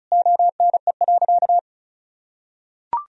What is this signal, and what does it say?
Morse code, heard as a keyed tone, about 36 wpm, 700 Hz. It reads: ONE.
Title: Sentences from Top 1000 Words - Review Part 2 - 35wpm